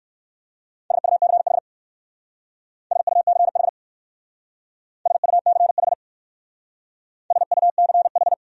Morse code, heard as a keyed tone, about 56 wpm, 690 Hz. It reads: SUCH SUCH SUCH SUCH
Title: Speed-Racing Top 500 Words 45wpm